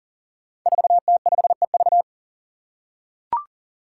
Morse code, heard as a keyed tone, about 40 wpm, 700 Hz. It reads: 4T5EV
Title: Five Challenging Letter-Number Combinations 40wpm